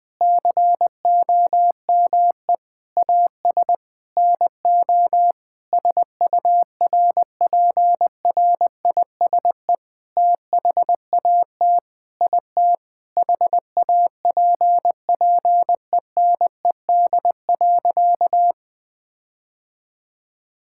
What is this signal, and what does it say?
Morse code, heard as a keyed tone, about 20 wpm, 700 Hz. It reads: COME AS NO SURPRISE THAT IT HAPPENED.
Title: Sentences with 8-Word Phrases - 20wpm